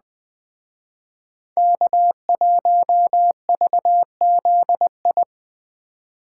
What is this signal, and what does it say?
Morse code, heard as a keyed tone, about 20 wpm, 700 Hz. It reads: K1VZI
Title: Speed Racing - USA 1x3 Call Signs 17wpm